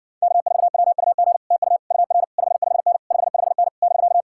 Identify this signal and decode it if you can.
Morse code, heard as a keyed tone, about 69 wpm, 700 Hz. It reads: K4WUG TU UR 55N 55N <BK>